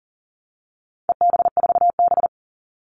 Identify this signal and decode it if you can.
Morse code, heard as a keyed tone, about 40 wpm, 700 Hz. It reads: EB4B
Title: All Cut Numbers - 4-Digit 40wpm